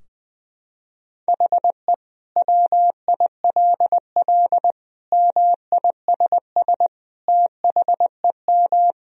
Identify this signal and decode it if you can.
Morse code, heard as a keyed tone, about 20 wpm, 700 Hz. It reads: HE WILL MISS THEM
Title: Sentences from Top 1000 Words - Review Part 3 - 20wpm